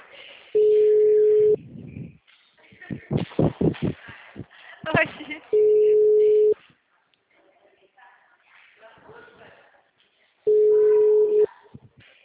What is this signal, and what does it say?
Ringback tone (Europe, 425 Hz), with speech.